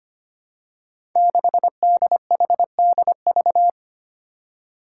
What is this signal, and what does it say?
Morse code, heard as a keyed tone, about 25 wpm, 700 Hz. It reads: 6DHDV
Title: Five Challenging Letter-Number Combinations 25wpm